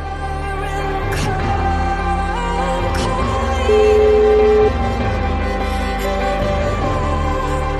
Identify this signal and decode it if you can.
Ringback tone (Europe, 425 Hz), with noise.